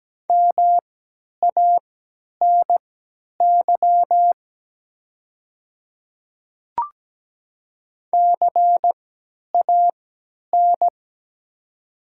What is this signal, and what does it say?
Morse code, heard as a keyed tone, about 17 wpm, spaced as for 10 wpm, 700 Hz. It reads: MANY CAN
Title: 3x Character Spacing - Top 100 Words - 17wpm